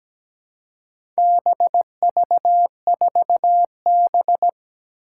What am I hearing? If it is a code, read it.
Morse code, heard as a keyed tone, about 17 wpm, 700 Hz. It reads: BV4B
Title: Four Challenging Letter-Number Combinations 17wpm